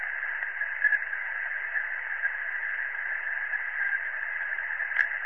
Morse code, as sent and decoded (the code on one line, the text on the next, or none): none